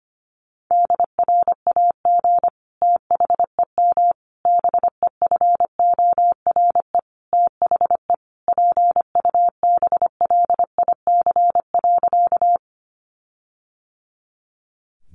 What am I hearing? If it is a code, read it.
Morse code, heard as a keyed tone, about 25 wpm, 700 Hz. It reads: DRAG THEM BEFORE THE PUBLIC.
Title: Sentences from Top 1500 Words 25wpm